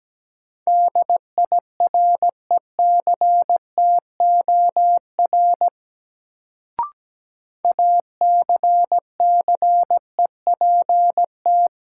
Morse code, heard as a keyed tone, about 17 wpm, 700 Hz. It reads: DIRECTOR ACCEPT
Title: Sentences from Top 1300 Words - New Word Drill Alt - 17wpm